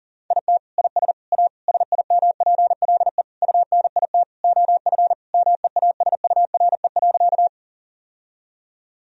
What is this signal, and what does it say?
Morse code, heard as a keyed tone, about 37 wpm, 700 Hz. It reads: IT IS A SIMPLE UNIT OF MEASURE.
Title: Sentences from Top 1000 Words - Review Part 2 - 40wpm